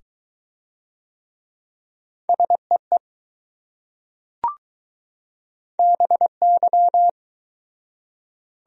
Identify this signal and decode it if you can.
Morse code, heard as a keyed tone, about 20 wpm, 700 Hz. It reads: SEE BY